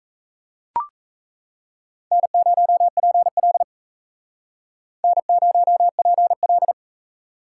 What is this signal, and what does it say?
Morse code, heard as a keyed tone, about 36 wpm, 700 Hz. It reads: N0PL N0PL